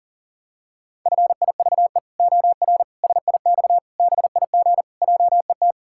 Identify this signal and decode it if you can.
Morse code, heard as a keyed tone, about 37 wpm, 700 Hz. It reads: FIVE OR SIX BIG JET
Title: Pangram Challenge v2 - 40wpm